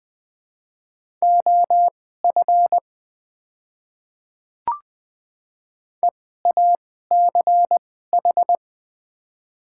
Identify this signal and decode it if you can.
Morse code, heard as a keyed tone, about 21 wpm, spaced as for 15 wpm, 700 Hz. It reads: OF EACH
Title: 2x Character Spacing - Top 100 Words - 20wpm